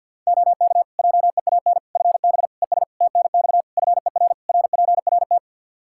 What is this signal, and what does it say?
Morse code, heard as a keyed tone, about 50 wpm, 700 Hz. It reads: OK JEAN FB ES TNX FER RPRT
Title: Practice QSO - Exchange 2 - Info - 50wpm